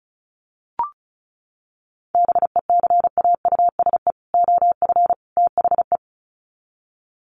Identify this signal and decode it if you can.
Morse code, heard as a keyed tone, about 34 wpm, 700 Hz. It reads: BECAUSE OF THE
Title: Sets of 3 Words 35wpm